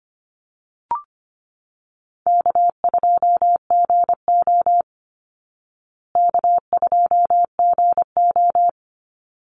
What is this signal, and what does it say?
Morse code, heard as a keyed tone, about 25 wpm, 700 Hz. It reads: K2GO K2GO
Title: Call Signs Repeated 3x - 25wpm